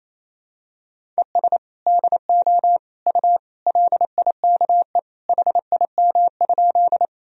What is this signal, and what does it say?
Morse code, heard as a keyed tone, about 27 wpm, 700 Hz. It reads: ES DO U LIKE HIM?
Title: Sentences from Top 100 Words and Abbreviations 28wpm